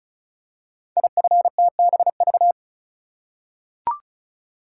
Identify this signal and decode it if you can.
Morse code, heard as a keyed tone, about 36 wpm, 700 Hz. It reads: IFTBV
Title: Code Groups - 5 Letters - 35wpm